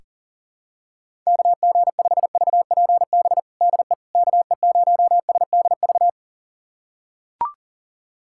Morse code, heard as a keyed tone, about 40 wpm, 700 Hz. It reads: KG5VPB DE KE0SDV
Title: USA Callsign DE Callsign 40wpm